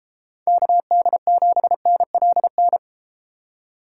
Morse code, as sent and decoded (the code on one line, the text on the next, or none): -.- -.. --... -. .-.. -.
KD7NLN